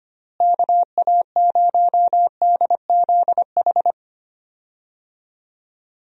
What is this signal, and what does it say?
Morse code, heard as a keyed tone, about 25 wpm, 700 Hz. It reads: KA0DZH
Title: USA Call Signs - 6 Characters - Repeated 3x - 25wpm